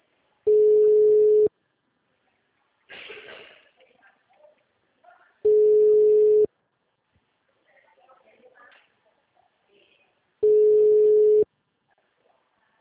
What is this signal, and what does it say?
Ringback tone (Europe, 425 Hz), with speech.